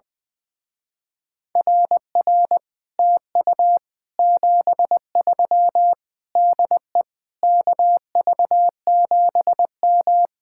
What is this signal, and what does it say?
Morse code, heard as a keyed tone, about 21 wpm, 700 Hz. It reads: RR TU 73 DE KV7M
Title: Speed-Racing Common QSO Phrases - 20wpm